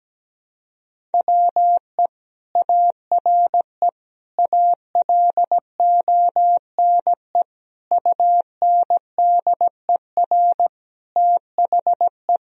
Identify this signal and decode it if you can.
Morse code, heard as a keyed tone, about 17 wpm, 700 Hz. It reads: WE ARE ALONE UNDER THE
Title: Sentences from Top 1000 Words - Review Part 1 - 17wpm